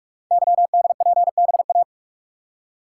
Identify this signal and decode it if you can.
Morse code, heard as a keyed tone, about 44 wpm, 690 Hz. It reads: YDPBA
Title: Code Groups - 5 Letters - 45wpm